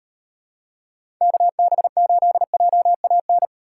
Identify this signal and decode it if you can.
Morse code, heard as a keyed tone, about 35 wpm, 700 Hz. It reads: KB8JAN